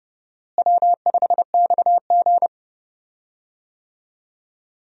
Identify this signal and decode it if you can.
Morse code, heard as a keyed tone, about 32 wpm, 700 Hz. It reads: W5XG
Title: USA Call Signs - 4 Characters - Repeated 2x - 30wpm